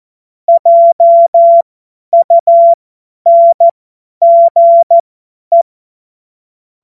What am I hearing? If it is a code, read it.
Morse code, heard as a keyed tone, about 14 wpm, spaced as for 10 wpm, 680 Hz. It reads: JUNGE
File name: Morsekurs / Witze / Witz 06. September 2022 50 F70